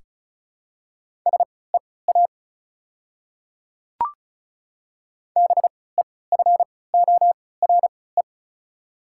Morse code, heard as a keyed tone, about 36 wpm, spaced as for 20 wpm, 700 Hz. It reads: SEA BEFORE
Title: 3x Character Spacing - Top 500 Words - 35wpm